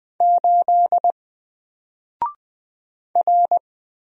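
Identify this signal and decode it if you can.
Morse code, heard as a keyed tone, about 20 wpm, 700 Hz. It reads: 8 R
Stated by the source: Single letter and number Morse drill at 20 wpm, from N4LCD.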